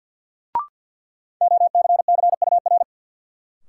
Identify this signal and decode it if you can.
Morse code, heard as a keyed tone, about 53 wpm, 700 Hz. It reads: OCCUR
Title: Sentences from Top 900 Words - New Word Drill - 50wpm